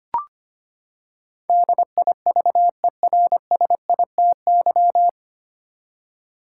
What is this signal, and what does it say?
Morse code, heard as a keyed tone, about 25 wpm, 700 Hz. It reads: DIVERSITY
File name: Nine Letter Words - 25wpm